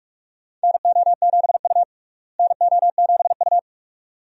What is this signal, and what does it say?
Morse code, heard as a keyed tone, about 46 wpm, 700 Hz. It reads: NO7U NO7U
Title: USA Call Signs - 4 Characters - Repeated 2x - 45wpm